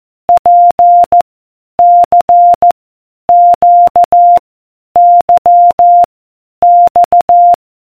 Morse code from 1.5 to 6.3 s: C Q Y